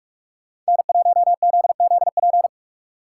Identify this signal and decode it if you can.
Morse code, heard as a keyed tone, about 59 wpm, spaced as for 52 wpm, 700 Hz. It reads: N1ZZP
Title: Speed Racing - USA 1x3 Call Signs 45wpm